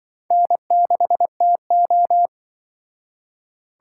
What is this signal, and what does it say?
Morse code, heard as a keyed tone, about 24 wpm, 700 Hz. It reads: N6TO
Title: Speed Racing - USA 1x2 Call Signs 20wpm